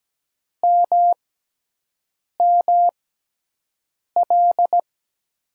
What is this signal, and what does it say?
Morse code, heard as a keyed tone, about 17 wpm, spaced as for 6 wpm, 700 Hz. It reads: MML
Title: USA Call Signs - 5 Characters - 6x Character Spacing - 17wpm